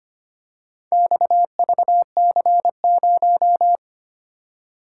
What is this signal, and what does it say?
Morse code, heard as a keyed tone, about 25 wpm, 700 Hz. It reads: XVC0
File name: practice / 25wpm / Four Letter-Number Combinations - 25wpm